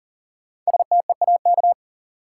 Morse code, heard as a keyed tone, about 37 wpm, 700 Hz. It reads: STEAK
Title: Five Letter Words - 40wpm